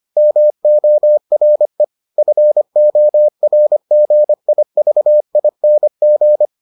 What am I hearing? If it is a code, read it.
Morse code, heard as a keyed tone, about 25 wpm, 600 Hz. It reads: MORE FORGIVING